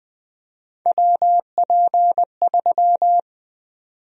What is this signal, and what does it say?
Morse code, heard as a keyed tone, about 20 wpm, 700 Hz. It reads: WP3
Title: USA Call Sign Prefixes - 20wpm